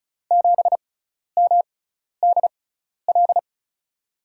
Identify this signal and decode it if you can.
Morse code, heard as a keyed tone, about 35 wpm, spaced as for 12 wpm, 700 Hz. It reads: 7MDL
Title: USA Call Signs - 5 Characters - 6x Character Spacing - 35wpm